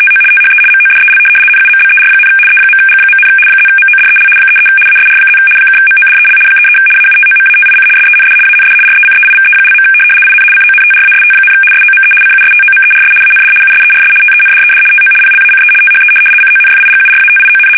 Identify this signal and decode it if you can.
RTTY, about 75 baud. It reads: TO THE MODEL NUMBER, E.G., +MODEL 33 ASR+ (MODEL 33 AUTOMATIC SEND AND RECEIVE). IN CONTRAST, SOME CUSTOMERS AND USERS TENDED TO PLACE THE CONFIGURATION
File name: fsk850-75B / teletype-corp